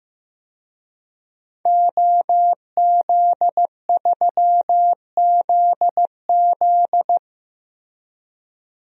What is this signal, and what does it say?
Morse code, heard as a keyed tone, about 15 wpm, 700 Hz. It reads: OZ3ZZ
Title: DXCC-100 Call Signs - 5 Characters - Repeated 3x - 15wpm